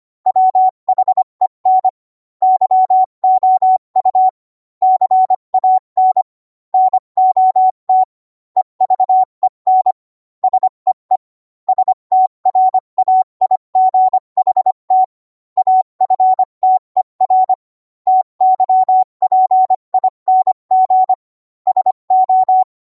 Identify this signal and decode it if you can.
Morse code, heard as a keyed tone, about 25 wpm, 750 Hz. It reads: WHEN YOU CAN NOT EVEN SEE STRAIGHT AFTER TYPING SO